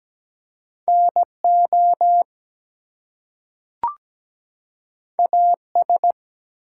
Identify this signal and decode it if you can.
Morse code, heard as a keyed tone, about 17 wpm, 700 Hz. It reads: NO AS
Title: Two Letter Words - 17wpm